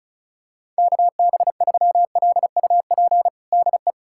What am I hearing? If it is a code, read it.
Morse code, heard as a keyed tone, about 36 wpm, 690 Hz. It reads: KB3LUP DE
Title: USA Callsign DE Callsign 35wpm